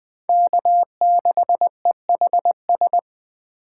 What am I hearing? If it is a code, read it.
Morse code, heard as a keyed tone, about 23 wpm, 700 Hz. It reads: K6EHS
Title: Speed Racing - USA 1x3 Call Signs 20wpm